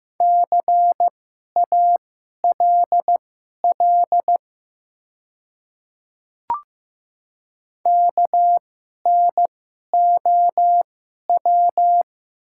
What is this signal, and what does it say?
Morse code, heard as a keyed tone, about 15 wpm, spaced as for 11 wpm, 700 Hz. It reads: CALL KNOW
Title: 2x Character Spacing - Top 100 Words - 15wpm